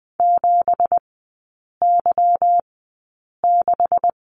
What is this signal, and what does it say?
Morse code, heard as a keyed tone, about 20 wpm, spaced as for 8 wpm, 700 Hz. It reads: 7Y6
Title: 218 Lesson 35e - Number 0 - Three Characters 20wpm